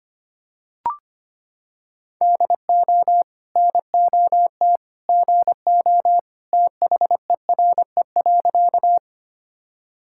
Morse code, heard as a keyed tone, about 25 wpm, 700 Hz. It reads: DO NOT GO THERE.
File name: practice / 25wpm / Sentences with 2-Word Phrases - 25wpm